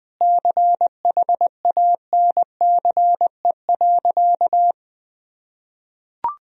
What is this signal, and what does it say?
Morse code, heard as a keyed tone, about 20 wpm, 700 Hz. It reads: CHANCE.
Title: Sentences from Top 1000 Words - Review Part 4 - 20wpm